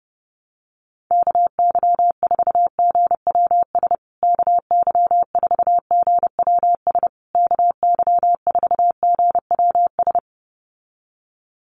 Morse code, heard as a keyed tone, about 30 wpm, 700 Hz. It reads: KY4GWS KY4GWS KY4GWS
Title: USA Call Signs Repeated 3x 30wpm